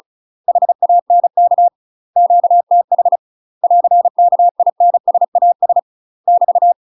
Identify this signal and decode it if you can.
Morse code, heard as a keyed tone, about 36 wpm, 700 Hz. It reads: HANK QTH <AR> KINSAS <BT>
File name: UK MASTER QSOs TRAINING/35WPM